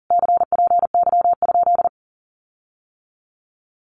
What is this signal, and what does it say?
Morse code, heard as a keyed tone, about 40 wpm, 700 Hz. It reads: CPY?
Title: Common QSO Phrases Encore 40wpm